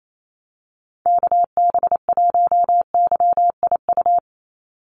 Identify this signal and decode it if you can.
Morse code, heard as a keyed tone, about 28 wpm, 700 Hz. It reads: KB1YIU
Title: USA Call Signs 28wpm